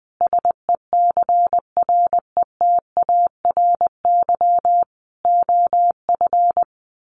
Morse code, heard as a keyed tone, about 20 wpm, 700 Hz. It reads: SECRETARY OF
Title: Sentences from Top 1400 Words 20wpm